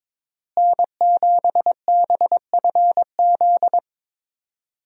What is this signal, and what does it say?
Morse code, heard as a keyed tone, about 22 wpm, 700 Hz. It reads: N7BFZ
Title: USA Call Signs - 5 Characters - Repeated 2x - 22wpm